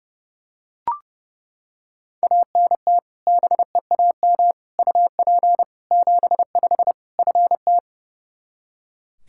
Morse code, heard as a keyed tone, about 30 wpm, 700 Hz. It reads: ANT BEAM UP 75 FT
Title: Practice QSO - Exchange 2 - Info - 30wpm